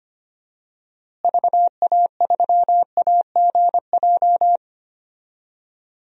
Morse code, heard as a keyed tone, about 25 wpm, 700 Hz. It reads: VA3AGJ
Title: DXCC-100 Call Signs - 6 Characters - Repeated 3x - 25wpm